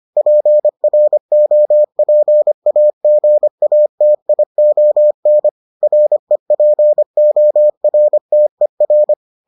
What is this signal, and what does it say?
Morse code, heard as a keyed tone, about 25 wpm, 600 Hz. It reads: PROPAGATION REPORTER